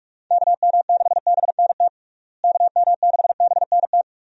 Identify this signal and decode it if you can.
Morse code, heard as a keyed tone, about 43 wpm, 700 Hz. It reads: KM6BNT KM6BNT
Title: USA Call Signs - 6 Characters - Repeated 2x - 45wpm